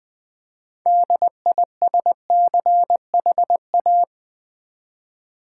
Morse code, heard as a keyed tone, about 20 wpm, 700 Hz. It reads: DISCHA
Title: Build A Word - 3 Letters At A Time - 9-Letter Words - 20wpm